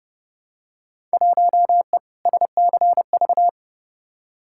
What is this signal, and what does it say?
Morse code, heard as a keyed tone, about 32 wpm, 700 Hz. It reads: 1E SCV
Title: Field Day Exchanges - 30wpm